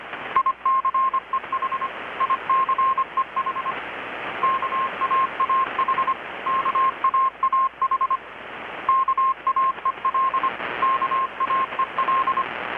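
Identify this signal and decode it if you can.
Morse code, heard as a keyed tone, about 25 wpm, 1060 Hz, with radio noise. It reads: ICEH ICEH KAAH KAAH KAEL KAEL
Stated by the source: Morse from KPH's Night of Nights XX maritime radiotelegraphy event, received in Chiapas, Mexico about 3,800 km away on a Kenwood R-600 with a grounded 10 m long-wire antenna.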